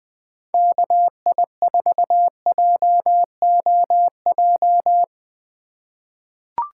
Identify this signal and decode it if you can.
Morse code, heard as a keyed tone, about 21 wpm, 700 Hz. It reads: KI4JOJ
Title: USA Call Signs - 6 Characters - 3x Character Spacing - 20wpm